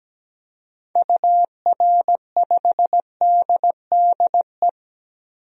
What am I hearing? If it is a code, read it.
Morse code, heard as a keyed tone, about 17 wpm, 700 Hz. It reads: UR5DDE
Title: DXCC-100 Call Signs - 6 Characters - Repeated 3x - 17wpm